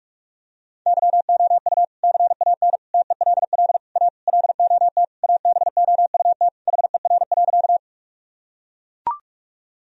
Morse code, heard as a keyed tone, about 42 wpm, 700 Hz. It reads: YOU CAN TELL A LOT ABOUT HER.
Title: Sentences from Top 1000 Words - Review Part 3 - 45wpm